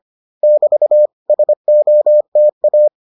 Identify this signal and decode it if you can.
Morse code, heard as a keyed tone, about 25 wpm, 600 Hz. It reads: <BT> SOTA